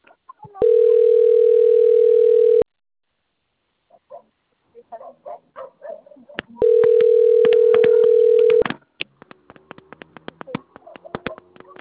Ringback tone (North America), 440 + 480 Hz.
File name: rg-1600-unknown- - - .99530